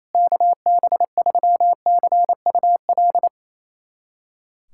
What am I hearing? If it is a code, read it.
Morse code, heard as a keyed tone, about 28 wpm, 700 Hz. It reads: KB3CUL
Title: USA Callsign DE Callsign 28wpm